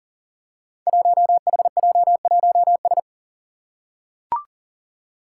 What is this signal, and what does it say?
Morse code, heard as a keyed tone, about 40 wpm, 700 Hz. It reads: 1HJ1S
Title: Five Challenging Letter-Number Combinations 40wpm